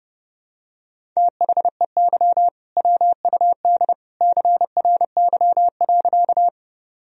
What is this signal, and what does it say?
Morse code, heard as a keyed tone, about 32 wpm, 700 Hz. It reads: THEY WUD CRY.
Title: Sentences from Top 500 Words and Abbreviations 30wpm